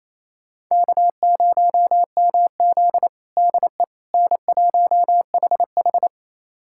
Morse code, heard as a keyed tone, about 28 wpm, 700 Hz. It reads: K0MZ DE N1HH
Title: USA Callsign DE Callsign - 4 Characters - 28wpm